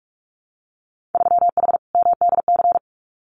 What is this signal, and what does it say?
Morse code, heard as a keyed tone, about 46 wpm, 690 Hz. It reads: 3H MDC
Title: Winter Field Day Exchanges Encore 45wpm